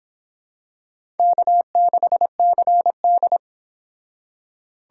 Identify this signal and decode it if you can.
Morse code, heard as a keyed tone, about 25 wpm, 700 Hz. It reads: K6CD